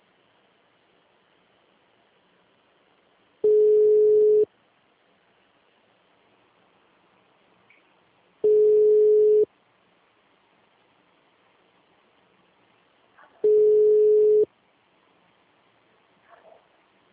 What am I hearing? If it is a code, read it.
Ringback tone (Europe, 425 Hz).